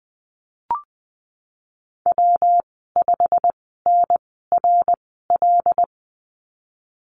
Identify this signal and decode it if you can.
Morse code, heard as a keyed tone, about 19 wpm, spaced as for 14 wpm, 700 Hz. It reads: W5NRL
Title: 128 Lesson 21g - Number 6 - Call Signs 20wpm